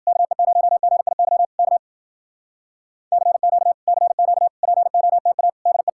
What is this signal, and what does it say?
Morse code, heard as a keyed tone, about 53 wpm, 700 Hz. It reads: KE0GIY K CQ CQ POTA DE